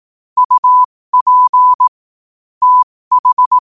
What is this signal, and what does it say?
Morse code, heard as a keyed tone, about 18 wpm, spaced as for 15 wpm, 1000 Hz. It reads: UP TH
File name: 15wpm